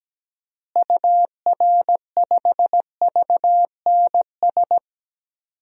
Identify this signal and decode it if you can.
Morse code, heard as a keyed tone, about 17 wpm, 700 Hz. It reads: UR5VNS
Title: DXCC-100 Call Signs - 6 Characters - Repeated 2x - 17wpm